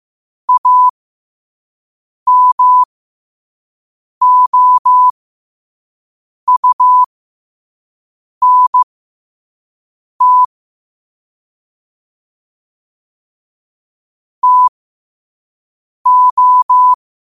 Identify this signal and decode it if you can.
Morse code, heard as a keyed tone, about 15 wpm, spaced as for 5 wpm, 1000 Hz. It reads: AMOUNT TO
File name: qotdfiles / 5wpm